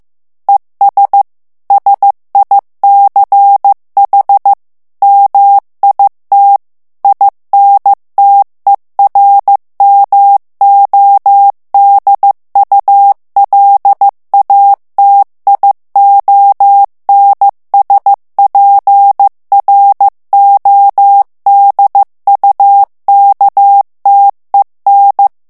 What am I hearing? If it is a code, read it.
Morse code, heard as a keyed tone, about 15 wpm, 800 Hz. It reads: ES SICH MIT INTERMODULATIONSPRODUKTEN
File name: morse mp3 / ALLEMAND texte clair 1 vitesse 14 mots minute